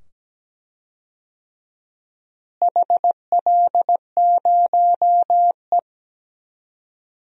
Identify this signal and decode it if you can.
Morse code, heard as a keyed tone, about 17 wpm, 700 Hz. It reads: HL0E
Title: DXCC-100 Call Signs - 4 Characters - Repeated 2x - 17wpm